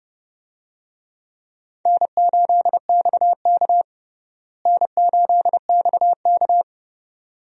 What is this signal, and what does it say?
Morse code, heard as a keyed tone, about 30 wpm, 700 Hz. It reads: N8XK N8XK
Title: USA Call Signs - 4 Characters - Repeated 2x - 30wpm